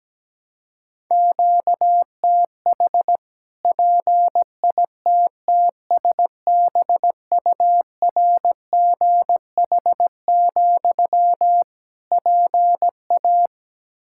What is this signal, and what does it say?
Morse code, heard as a keyed tone, about 17 wpm, 700 Hz. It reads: QTH PITTSBURGH, PA
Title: Practice QSO - Exchange 1 - Beginning - 17wpm